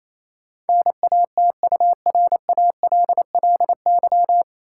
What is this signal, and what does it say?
Morse code, heard as a keyed tone, about 27 wpm, 700 Hz. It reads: NATURALLY
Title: Sentences from Top 1300 Words - New Word Drill - 28wpm